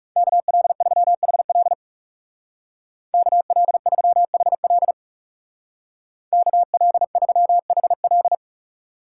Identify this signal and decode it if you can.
Morse code, heard as a keyed tone, about 36 wpm, 700 Hz. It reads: KL3HL KL3HL KL3HL
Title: Speed Racing - USA 2x2 Call Signs 30wpm